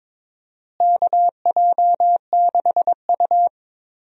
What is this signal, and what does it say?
Morse code, heard as a keyed tone, about 21 wpm, 700 Hz. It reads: KJ6U
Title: USA Call Signs - 4 Characters - Repeated 2x - 22wpm